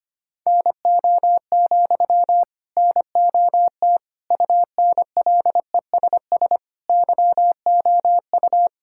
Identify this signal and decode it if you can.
Morse code, heard as a keyed tone, about 25 wpm, 700 Hz. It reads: NO, NOT UNLESS YOU
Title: Sentences from Top 1000 Words - Review Part 6 - 25wpm